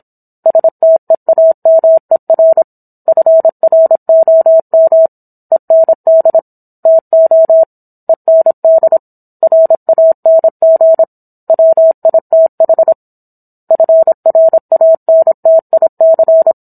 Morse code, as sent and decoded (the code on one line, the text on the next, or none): ... - . .- -- . .-. / ..-. .-. --- -- / . -. -.. / - --- / . -. -.. / .-. .- -. --. / .-- .. - .... / ..-. .-. .- -. - .. -.-.
STEAMER FROM END TO END RANG WITH FRANTIC